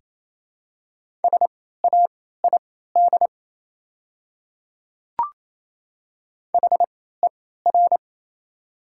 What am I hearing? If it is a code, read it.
Morse code, heard as a keyed tone, about 27 wpm, spaced as for 16 wpm, 700 Hz. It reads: SAID HER
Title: 3x Character Spacing - Top 300 Words - 28wpm